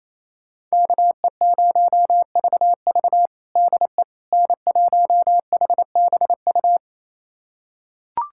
Morse code, heard as a keyed tone, about 27 wpm, 700 Hz. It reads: KE0VV DE N1HBU
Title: USA Callsign DE Callsign - 5 Characters - 28wpm